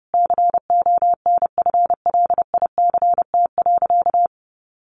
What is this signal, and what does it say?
Morse code, heard as a keyed tone, about 30 wpm, 700 Hz. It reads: CONFLICT.
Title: Sentences from Top 1300 Words 30wpm